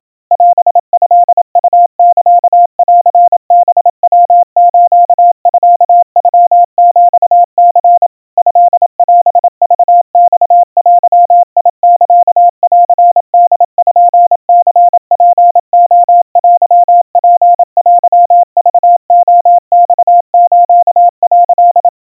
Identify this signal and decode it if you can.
Morse code (Japanese Wabun code), heard as a keyed tone, about 27 wpm, 700 Hz. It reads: オトウサンハヤスミノヒニ トオクマデサンポニツレテツテクレマス」
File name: kawa / Morse sample / kamyu 100 cpm 700 Hz Crispy